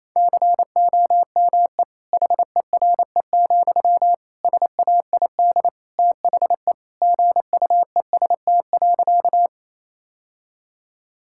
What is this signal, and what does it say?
Morse code, heard as a keyed tone, about 28 wpm, 700 Hz. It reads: COME HERE, SAID THE GUEST.
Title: Sentences from Top 1600 Words 28wpm